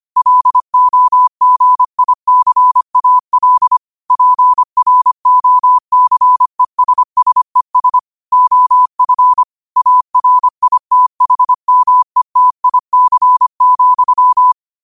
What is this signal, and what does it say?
Morse code, heard as a keyed tone, about 25 wpm, 1000 Hz. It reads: LOGICAL PROCESSES OF ARITHMETIC,
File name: qotdfiles / 25wpm